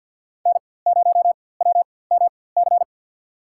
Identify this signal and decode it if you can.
Morse code, heard as a keyed tone, about 50 wpm, spaced as for 23 wpm, 700 Hz. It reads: N0WMC
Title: USA Call Signs - 5 Characters - 4x Character Spacing - 50wpm